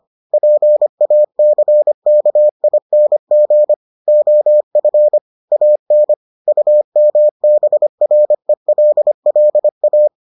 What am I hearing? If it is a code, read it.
Morse code, heard as a keyed tone, about 25 wpm, 600 Hz. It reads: PACKING OF AN UMBRELLA